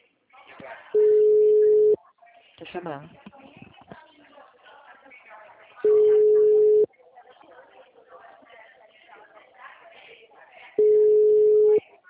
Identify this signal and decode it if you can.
Ringback tone (Europe, 425 Hz), with speech.